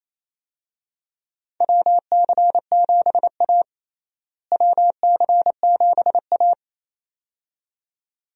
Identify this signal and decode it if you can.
Morse code, heard as a keyed tone, about 28 wpm, 700 Hz. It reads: WC7A WC7A
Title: USA Call Signs - 4 Characters - Repeated 2x - 28wpm